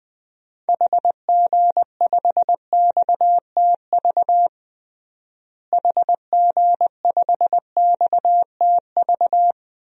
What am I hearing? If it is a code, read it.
Morse code, heard as a keyed tone, about 20 wpm, 700 Hz. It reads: HG5XTV HG5XTV
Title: DXCC-100 Call Signs - 6 Characters - Repeated 2x - 20wpm